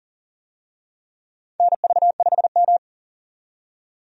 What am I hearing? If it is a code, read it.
Morse code, heard as a keyed tone, about 40 wpm, 700 Hz. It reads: NV5M
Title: USA Call Signs - 4 Characters - 40wpm